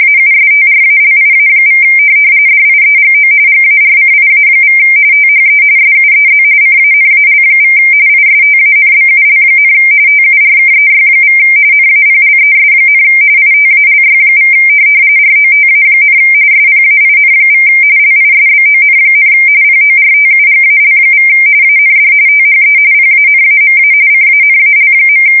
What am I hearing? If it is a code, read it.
RTTY, about 45.45 baud. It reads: AUSTRALIA= DIEGO GARCIA= THE SUEZ CANAL= ISTANBUL, TURKEY= NAPLES, ITALY= ROTA, SPAIN= LISBON, PORTUGAL= AND THE PANAMA CANAL.
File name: fsk170 / USS-Missouri